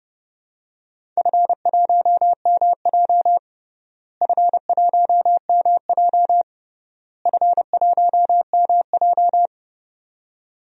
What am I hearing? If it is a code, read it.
Morse code, heard as a keyed tone, about 32 wpm, 700 Hz. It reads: F1MJ F1MJ F1MJ
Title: DXCC-100 Call Signs - 4 Characters - Repeated 3x - 30wpm